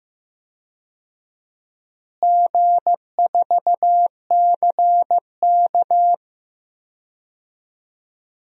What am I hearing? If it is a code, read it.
Morse code, heard as a keyed tone, about 15 wpm, 700 Hz. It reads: G4CK
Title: DXCC-100 Call Signs - 4 Characters - Repeated 3x - 15wpm